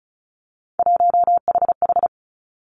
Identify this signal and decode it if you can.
Morse code, heard as a keyed tone, about 34 wpm, 700 Hz. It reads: 1HH
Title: Three Challenging Letter-Number Combinations 35wpm